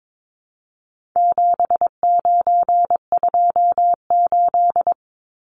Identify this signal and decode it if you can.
Morse code, heard as a keyed tone, about 22 wpm, 700 Hz. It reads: 7928
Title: Numbers - 4-Digit 22wpm